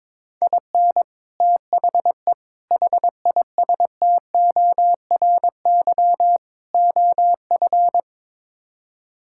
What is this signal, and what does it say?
Morse code, heard as a keyed tone, about 21 wpm, 700 Hz. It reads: IN THE HISTORY OF
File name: practice / 22wpm / Build A Phrase - 5-Word Phrases - 22wpm